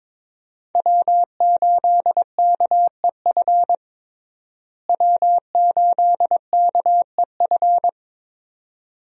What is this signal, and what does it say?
Morse code, heard as a keyed tone, about 22 wpm, 700 Hz. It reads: W8KEF W8KEF
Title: USA Call Signs - 5 Characters - Repeated 2x - 22wpm